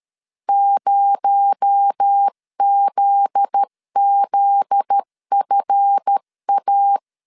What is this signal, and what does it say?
Morse code, heard as a keyed tone, about 13 wpm, 800 Hz. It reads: 0ZZFA
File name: LEZIONI / 7 GF9Z / 7 GF9Z 12wpm 2147